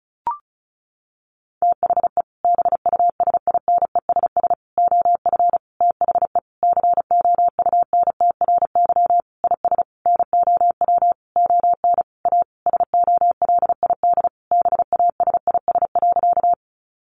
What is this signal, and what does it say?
Morse code, heard as a keyed tone, about 34 wpm, 700 Hz. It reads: THE BUSINESS OF THE COUNTRY IS NOW ON A SOLID BASIS.
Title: Sentences from Top 1500 Words 35wpm